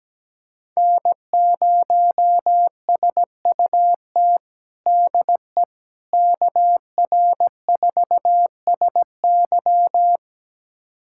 Morse code, heard as a keyed tone, about 17 wpm, 700 Hz. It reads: N0SUT DE KR4SY
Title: USA Callsign DE Callsign - 5 Characters - 17wpm